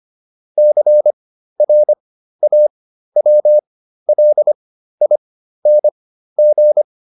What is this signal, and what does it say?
Morse code, heard as a keyed tone, about 25 wpm, spaced as for 13 wpm, 600 Hz. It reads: CRAWLING
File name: ING8 13